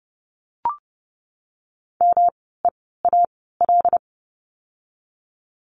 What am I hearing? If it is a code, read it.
Morse code, heard as a keyed tone, about 30 wpm, spaced as for 17 wpm, 700 Hz. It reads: MEAL
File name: course / 30:10wpm / 108 Lesson 18f - Letter m - Words 30wpm